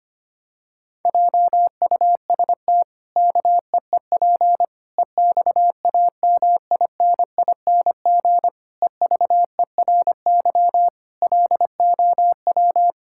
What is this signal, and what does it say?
Morse code, heard as a keyed tone, about 25 wpm, 700 Hz. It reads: JUST KEEP EXAMINING EVERY LOW
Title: Pangram Challenge v2 - 25wpm